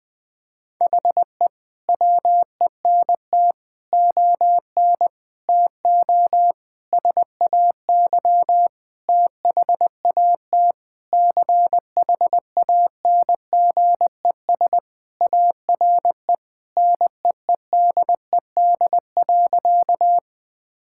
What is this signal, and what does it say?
Morse code, heard as a keyed tone, about 20 wpm, 700 Hz. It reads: HE WENT ON TO SAY THAT CHANGES ARE NEEDED.
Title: Sentences with 6-Word Phrases - 20wpm